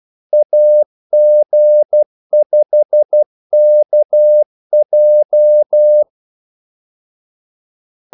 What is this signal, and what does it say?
Morse code, heard as a keyed tone, about 12 wpm, 600 Hz. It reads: AG5KJ
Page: 2 ekstra ordmellemrum mellem kaldesignalerne